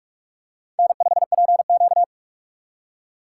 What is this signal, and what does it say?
Morse code, heard as a keyed tone, about 41 wpm, 700 Hz. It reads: N5PQ
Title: USA Call Signs - 4 Characters - 6x Character Spacing - 45wpm